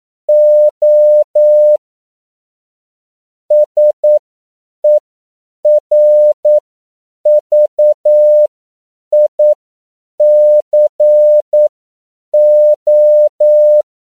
Morse code, heard as a keyed tone, about 9 wpm, spaced as for 7 wpm, 600 Hz. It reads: O SERVICO
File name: CW / 10PPM